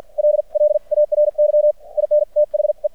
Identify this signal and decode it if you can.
Morse code, heard as a keyed tone, about 50 wpm, spaced as for 42 wpm, 600 Hz. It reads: LLARY IN THE
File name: archived / CRSnet